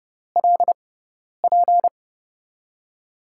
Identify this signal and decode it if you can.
Morse code, heard as a keyed tone, about 30 wpm, spaced as for 10 wpm, 700 Hz. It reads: LP
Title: USA Call Signs - 5 Characters - 6x Character Spacing - 30wpm